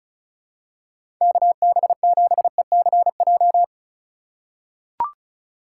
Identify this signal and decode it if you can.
Morse code, heard as a keyed tone, about 36 wpm, 700 Hz. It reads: KB7ECJ
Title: USA Call Signs - 4x Character Spacing - 35wpm